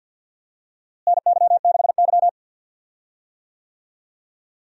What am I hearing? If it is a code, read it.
Morse code, heard as a keyed tone, about 52 wpm, 700 Hz. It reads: NY6Y
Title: USA Call Signs - 4 Characters - Repeated 3x - 50wpm